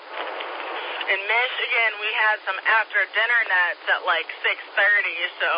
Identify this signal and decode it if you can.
Emergency Alert System (SAME) tones, heard as harsh buzzing bursts, with speech. It is heard from 1.0 to 4.8 s.